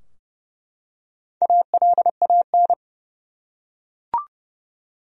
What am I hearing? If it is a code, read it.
Morse code, heard as a keyed tone, about 29 wpm, 700 Hz. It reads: ALAN
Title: USA Common Names - Max 4 Letters - 30wpm